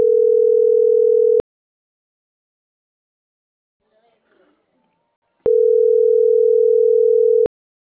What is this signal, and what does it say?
Ringback tone (North America), 440 + 480 Hz.